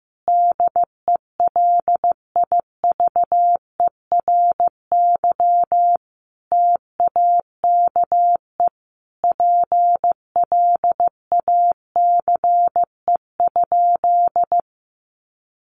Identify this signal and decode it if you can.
Morse code, heard as a keyed tone, about 15 wpm, 700 Hz. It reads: DELIVERY TAKE PLACE?
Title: Sentences from Top 1500 Words 15wpm